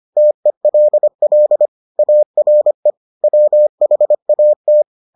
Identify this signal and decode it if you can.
Morse code, heard as a keyed tone, about 25 wpm, 600 Hz. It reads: TELL ARE WHAT